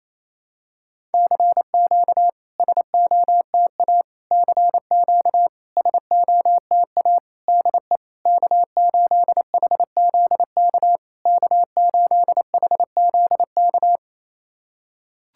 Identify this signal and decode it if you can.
Morse code, heard as a keyed tone, about 28 wpm, 700 Hz. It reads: CQ SOTA CQ SOTA DE K8HZK K8HZK
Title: SOTA QSO Practice 28wpm